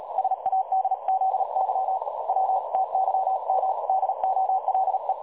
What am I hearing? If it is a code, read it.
Morse code, heard as a keyed tone, about 36 wpm, 760 Hz, with noise. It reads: SADZY W TYM DOLE
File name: dlazaawansowanych / nagraniazpasma / websdr recording 3565.0kHz